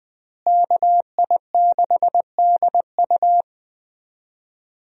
Morse code, heard as a keyed tone, about 21 wpm, 700 Hz. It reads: KI6DU
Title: USA Call Signs - 5 Characters - Repeated 2x - 20wpm